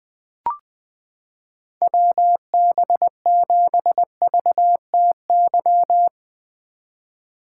Morse code, heard as a keyed tone, about 20 wpm, 700 Hz. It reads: WB7VTY
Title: 2023 SOTA Chaser Call Signs - 20wpm